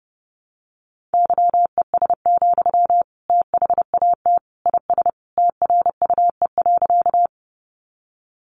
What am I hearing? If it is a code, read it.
Morse code, heard as a keyed tone, about 30 wpm, 700 Hz. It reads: YES, THAT IS TRUE.
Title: Sentences from Top 500 Words 30wpm